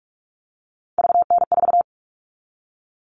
Morse code, heard as a keyed tone, about 44 wpm, 700 Hz. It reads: VN4
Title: All Cut Numbers - 3-Digit 45wpm